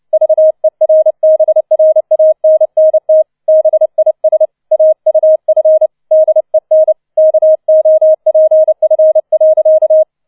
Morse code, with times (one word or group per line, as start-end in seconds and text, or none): 0.1-3.2 VERBRANNT
3.5-4.5 BIS
4.7-5.9 AUF
6.1-6.9 DEN
7.2-10.0 KOPF.